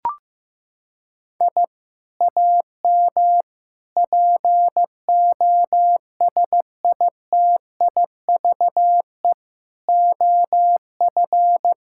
Morse code, heard as a keyed tone, about 16 wpm, 700 Hz. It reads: I AM POSITIVE OF
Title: Sentences from Top 1000 Words - Review Part 5 - 15wpm